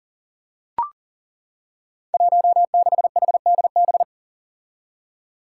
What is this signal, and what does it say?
Morse code, heard as a keyed tone, about 40 wpm, 700 Hz. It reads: 16HDB
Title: Five Challenging Letter-Number Combinations 40wpm